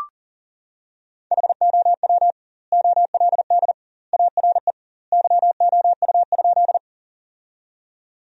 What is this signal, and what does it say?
Morse code, heard as a keyed tone, about 40 wpm, 700 Hz. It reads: HOW OLD ARE YOU?
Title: 2x Word-Spacing - Sentences from Top 200 Words - 40wpm